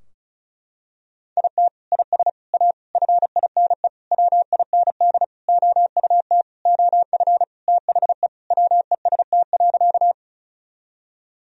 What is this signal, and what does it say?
Morse code, heard as a keyed tone, about 36 wpm, 700 Hz. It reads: IT IS A FINE WIND OUT OF THE WEST.
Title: Sentences from Top 1000 Words - Review Part 2 - 35wpm